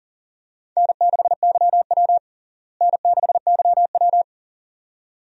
Morse code, heard as a keyed tone, about 40 wpm, 700 Hz. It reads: N6YW N6YW
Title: USA Call Signs - 4 Characters - Repeated 2x - 40wpm